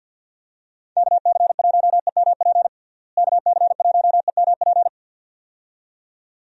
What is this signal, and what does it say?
Morse code, heard as a keyed tone, about 48 wpm, 700 Hz. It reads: KC1EMP KC1EMP
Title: USA Call Signs - 6 Characters - Repeated 2x - 50wpm